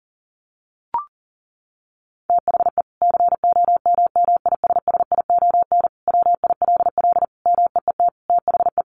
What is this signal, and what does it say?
Morse code, heard as a keyed tone, about 40 wpm, 700 Hz. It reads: THE COMMISSION WILL MEET THE
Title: Sentences from Top 1400 Words 40wpm